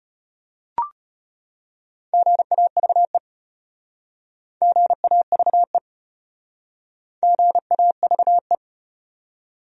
Morse code, heard as a keyed tone, about 34 wpm, 700 Hz. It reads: GAVE GAVE GAVE